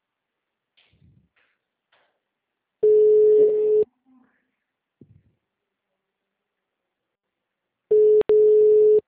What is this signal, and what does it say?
Ringback tone (Europe, 425 Hz), with speech.